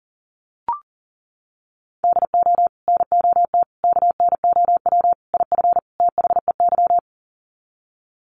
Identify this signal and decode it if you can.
Morse code, heard as a keyed tone, about 40 wpm, 700 Hz. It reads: DO NOT KNOW IF THEY
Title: Sets of 5 Words Encore 40wpm